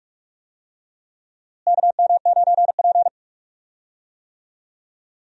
Morse code, heard as a keyed tone, about 41 wpm, 700 Hz. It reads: KM9P
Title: USA Call Signs - 4 Characters - Repeated 2x - 45wpm